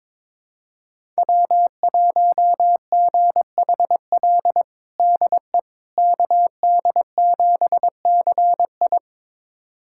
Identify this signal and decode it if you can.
Morse code, heard as a keyed tone, about 22 wpm, 700 Hz. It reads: W1GHL DE KD7CI
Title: USA Callsign DE Callsign - 5 Characters - 22wpm